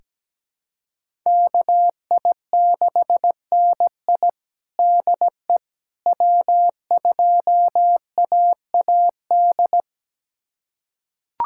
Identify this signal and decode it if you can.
Morse code, heard as a keyed tone, about 17 wpm, 700 Hz. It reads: KI6NI DE W2AAD
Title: USA Callsign DE Callsign - 5 Characters - 17wpm